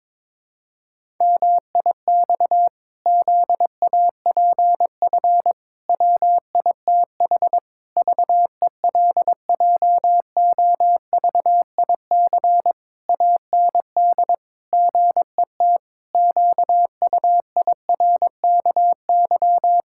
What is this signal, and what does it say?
Morse code, heard as a keyed tone, about 22 wpm, 700 Hz. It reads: MIX ZAPF WITH VELJOVIC AND GET QUIRKY
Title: Pangram Challenge v2 - 22wpm